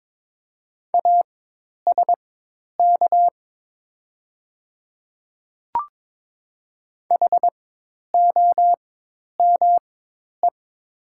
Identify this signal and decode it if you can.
Morse code, heard as a keyed tone, about 22 wpm, spaced as for 10 wpm, 700 Hz. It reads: ASK HOME
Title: 4x Character Spacing - Top 300 Words - 22wpm